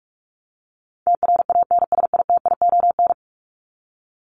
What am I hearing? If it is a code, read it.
Morse code, heard as a keyed tone, about 44 wpm, 700 Hz. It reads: TRANSITION
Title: Top 1751-2000 Words 45wpm